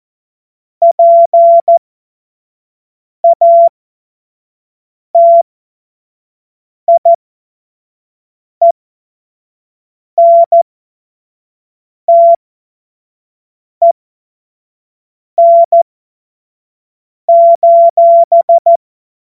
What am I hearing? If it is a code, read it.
Morse code, heard as a keyed tone, about 14 wpm, spaced as for 5 wpm, 680 Hz. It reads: PATIENTEN:
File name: Morsekurs / Witze / Witz 02 April 2024 25 F70